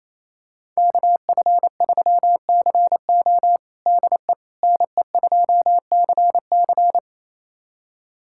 Morse code, heard as a keyed tone, about 28 wpm, 700 Hz. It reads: KF3CO DE NE2CC
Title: USA Callsign DE Callsign 28wpm